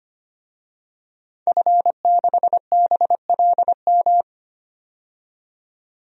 Morse code, heard as a keyed tone, about 25 wpm, 700 Hz. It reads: F6BLM
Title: DXCC-100 Call Signs - 5 Characters - 25wpm